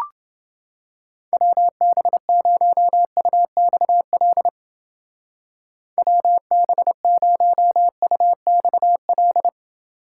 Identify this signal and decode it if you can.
Morse code, heard as a keyed tone, about 30 wpm, 700 Hz. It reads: WB0UXL WB0UXL